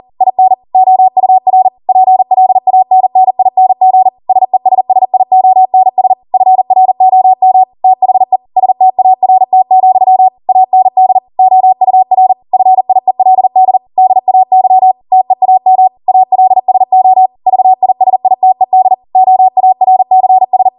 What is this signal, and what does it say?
Morse code, heard as a keyed tone, about 40 wpm, 750 Hz. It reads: IN OUR PLANNING SESSIONS FROM THE START, AND OUR FIELD DAY TEAM ALSO VISITED OARCS